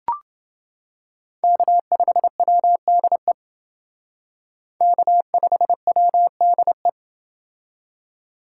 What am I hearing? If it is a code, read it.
Morse code, heard as a keyed tone, about 28 wpm, 700 Hz. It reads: K5WDE K5WDE